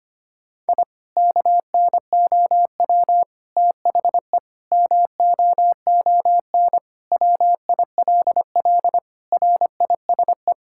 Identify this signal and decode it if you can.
Morse code, heard as a keyed tone, about 25 wpm, 700 Hz. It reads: I KNOW THE MOON WILL RISE
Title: Sentences from Top 1000 Words - Review Part 6 - 25wpm